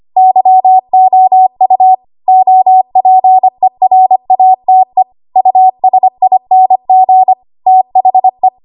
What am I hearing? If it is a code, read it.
Morse code, heard as a keyed tone, about 25 wpm, 750 Hz. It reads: YOU OPERATE USING THE
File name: Archive / 25 WPM / 25WPM